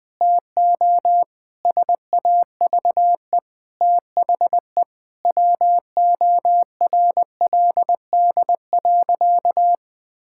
Morse code, heard as a keyed tone, about 21 wpm, 700 Hz. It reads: TO SAVE THE WORLD.
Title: Sentences from Top 1600 Words 20wpm